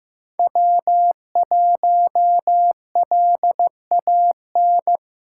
Morse code, heard as a keyed tone, about 15 wpm, 700 Hz. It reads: W1LAN
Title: USA Call Signs - 5 Characters - 2x Character Spacing - 15wpm